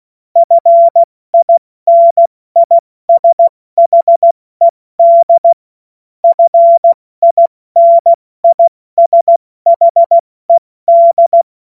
Morse code, heard as a keyed tone, about 16 wpm, 680 Hz. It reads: FINISHED FINISHED
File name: Morsekurs / Witze / Witz 27. September 2022 70 F80